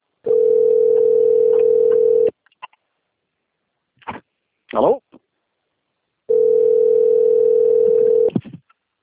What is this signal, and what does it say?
Ringback tone (North America), 440 + 480 Hz.
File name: rg-1600-unknown- - - .8630